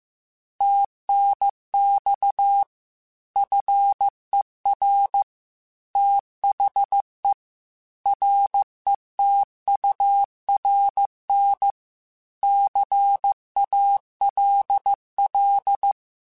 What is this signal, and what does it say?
Morse code, heard as a keyed tone, about 15 wpm, 800 Hz. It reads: TNX FER THE RETURN CALL